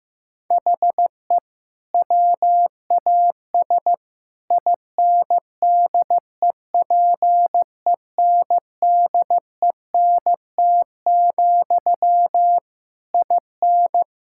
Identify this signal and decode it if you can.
Morse code, heard as a keyed tone, about 16 wpm, 700 Hz. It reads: HE WAS INDEPENDENT, IN
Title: Sentences from Top 1000 Words - Review Part 6 - 15wpm